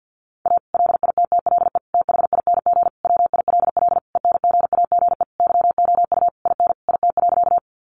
Morse code, heard as a keyed tone, about 50 wpm, 700 Hz. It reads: A LITTLE THING WILL ENGAGE YOU IN IT.
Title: Sentences from Top 1500 Words 50wpm